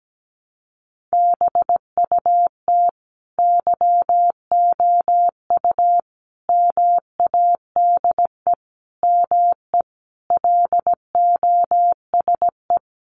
Morse code, heard as a keyed tone, about 17 wpm, 700 Hz. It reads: BUT YOU MADE ME LOSE
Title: Sentences from Top 1300 Words 17wpm